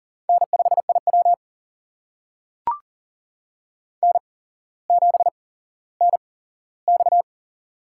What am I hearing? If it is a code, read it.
Morse code, heard as a keyed tone, about 40 wpm, spaced as for 17 wpm, 700 Hz. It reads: N5IW N7NX
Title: USA Call Signs - 4 Characters - 8x Character Spacing - 40wpm